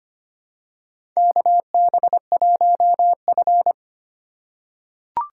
Morse code, heard as a keyed tone, about 25 wpm, 700 Hz. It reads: KB1F
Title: USA Call Signs - 4 Characters - 4x Character Spacing - 25wpm